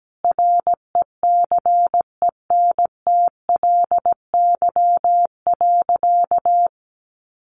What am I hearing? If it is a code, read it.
Morse code, heard as a keyed tone, about 17 wpm, 700 Hz. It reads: RECENTLY.
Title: Sentences from Top 1200 Words 17wpm